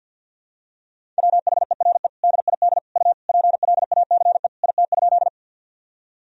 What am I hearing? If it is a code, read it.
Morse code, heard as a keyed tone, about 47 wpm, 700 Hz. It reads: WHERE DID U PLACE IT?
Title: Sentences from Top 200 Words and Abbreviations 50wpm